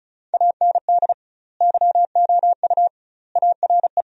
Morse code, heard as a keyed tone, about 36 wpm, 700 Hz. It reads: AND YOU ARE
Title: 2x Word-Spacing - Sentences from Top 400 Words - 35wpm